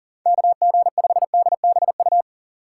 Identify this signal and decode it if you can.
Morse code, heard as a keyed tone, about 40 wpm, 700 Hz. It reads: KG5DBU
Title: USA Call Signs - 6 Characters - 3x Character Spacing - 40wpm